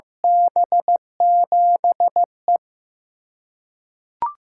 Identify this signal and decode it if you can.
Morse code, heard as a keyed tone, about 15 wpm, 700 Hz. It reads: B7E
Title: USA Call Signs - 6x Character Spacing - 15wpm